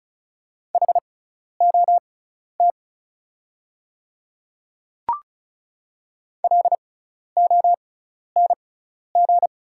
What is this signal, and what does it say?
Morse code, heard as a keyed tone, about 36 wpm, spaced as for 12 wpm, 700 Hz. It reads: HOT LONG
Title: 6x Character Spacing - Top 100 Words - 35wpm